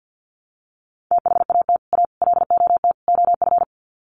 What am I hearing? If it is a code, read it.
Morse code, heard as a keyed tone, about 50 wpm, 700 Hz. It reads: THAT A LOT OF
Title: Sets of 4 Words 50wpm